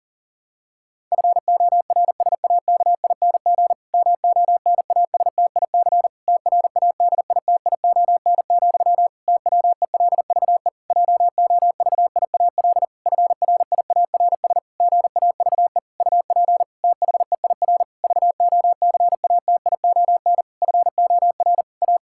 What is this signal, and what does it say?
Morse code, heard as a keyed tone, about 37 wpm, 700 Hz. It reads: FORSAKING MONASTIC TRADITION, TWELVE JOVIAL FRIARS GAVE UP THEIR VOCATION FOR A
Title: Pangram Challenge v2 - 40wpm